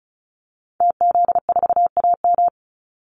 Morse code, heard as a keyed tone, about 34 wpm, 700 Hz. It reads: TZ4AM
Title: Call Signs - 35wpm